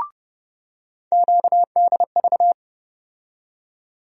Morse code, heard as a keyed tone, about 30 wpm, 700 Hz. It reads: QDV
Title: USA Call Sign Postfixes - 3 Letters 30wpm